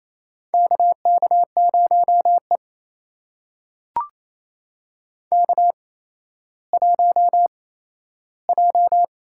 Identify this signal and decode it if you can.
Morse code, heard as a keyed tone, about 29 wpm, spaced as for 14 wpm, 700 Hz. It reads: KK0E K1J
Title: USA Call Signs - 4 Characters - 8x Character Spacing - 28wpm